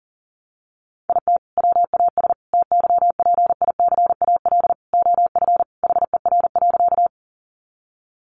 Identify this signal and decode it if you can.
Morse code, heard as a keyed tone, about 40 wpm, 700 Hz. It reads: IT WAS TYPICAL OF HER.
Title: Sentences from Top 1100 Words 40wpm